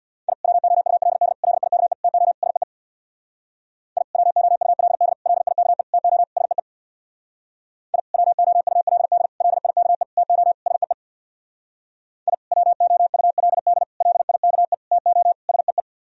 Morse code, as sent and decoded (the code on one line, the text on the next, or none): .. / .-- --- ..- .-.. -.. / .-.. .. -.- . / - --- / ... . / .. / .-- --- ..- .-.. -.. / .-.. .. -.- . / - --- / ... . / .. / .-- --- ..- .-.. -.. / .-.. .. -.- . / - --- / ... . / .. / .-- --- ..- .-.. -.. / .-.. .. -.- . / - --- / ... .
I WOULD LIKE TO SEE I WOULD LIKE TO SEE I WOULD LIKE TO SEE I WOULD LIKE TO SEE